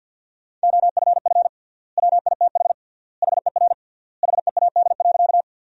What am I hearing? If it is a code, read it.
Morse code, heard as a keyed tone, about 48 wpm, 700 Hz. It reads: OFF WITH HER HEAD.
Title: 3x Word-Spacing - Sentences from Top 300 Words - 50wpm